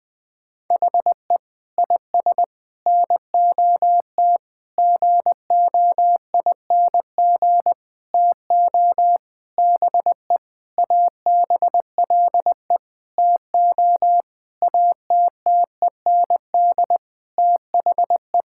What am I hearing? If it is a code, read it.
Morse code, heard as a keyed tone, about 20 wpm, 700 Hz. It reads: HE IS NOT GOING TO BE ABLE TO ATTEND THE
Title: Sentences with 6-Word Phrases - 20wpm